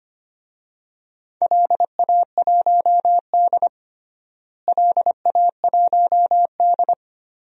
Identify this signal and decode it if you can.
Morse code, heard as a keyed tone, about 25 wpm, 700 Hz. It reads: LA1D LA1D
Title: DXCC-100 Call Signs - 4 Characters - Repeated 3x - 25wpm